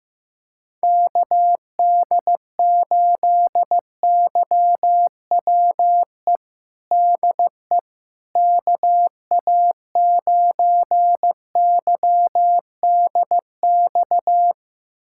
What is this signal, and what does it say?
Morse code, heard as a keyed tone, about 15 wpm, 700 Hz. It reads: KD8YWE DE KA9YDX
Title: USA Callsign DE Callsign 15wpm